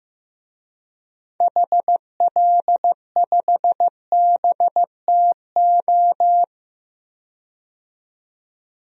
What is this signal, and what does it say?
Morse code, heard as a keyed tone, about 15 wpm, 700 Hz. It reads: HL5BTO
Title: DXCC-100 Call Signs - 6 Characters - 15wpm